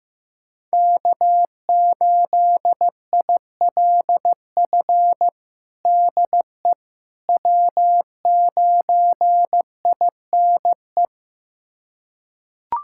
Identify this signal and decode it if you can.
Morse code, heard as a keyed tone, about 16 wpm, 700 Hz. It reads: K8ILF DE W9INE
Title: USA Callsign DE Callsign - 5 Characters - 15wpm